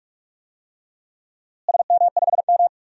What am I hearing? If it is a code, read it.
Morse code, heard as a keyed tone, about 48 wpm, 700 Hz. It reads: SM5M
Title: DXCC-100 Call Signs - 4 Characters - Repeated 3x - 45wpm